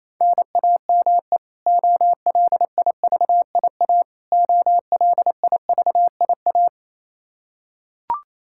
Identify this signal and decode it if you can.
Morse code, heard as a keyed tone, about 27 wpm, 700 Hz. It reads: NAME OLIVIA OLIVIA
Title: Practice QSO - Exchange 1 - Beginning - 28wpm